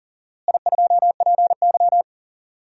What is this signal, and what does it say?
Morse code, heard as a keyed tone, about 37 wpm, 700 Hz. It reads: I2PY
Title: DXCC-100 Call Signs - 4 Characters - Repeated 3x - 40wpm